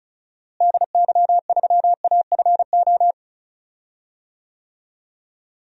Morse code, heard as a keyed tone, about 34 wpm, 700 Hz. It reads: DY3AFO
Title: DXCC-100 Call Signs - 6 Characters - Repeated 2x - 35wpm